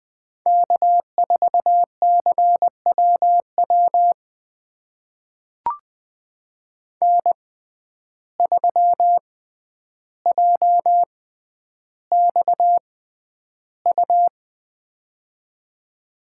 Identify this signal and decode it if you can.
Morse code, heard as a keyed tone, about 20 wpm, spaced as for 10 wpm, 700 Hz. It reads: K4CWW N3JXU
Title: USA Call Signs - 5 Characters - 6x Character Spacing - 20wpm